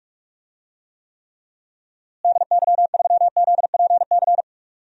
Morse code, heard as a keyed tone, about 46 wpm, 700 Hz. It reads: DY3ZPC
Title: DXCC-100 Call Signs - 6 Characters - Repeated 3x - 45wpm